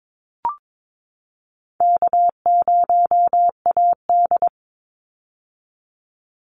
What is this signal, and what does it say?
Morse code, heard as a keyed tone, about 22 wpm, 700 Hz. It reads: K0AD
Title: Call Signs - 22wpm